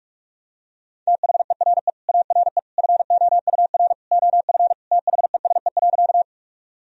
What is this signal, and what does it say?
Morse code, heard as a keyed tone, about 44 wpm, 700 Hz. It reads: THERE ARE FOUR OF THESE.
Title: Sentences from Top 1000 Words - Review Part 1 - 45wpm